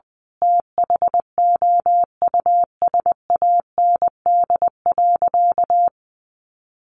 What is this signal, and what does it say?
Morse code, heard as a keyed tone, about 20 wpm, 700 Hz. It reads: THOUSAND.
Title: Sentences from Top 1300 Words 20wpm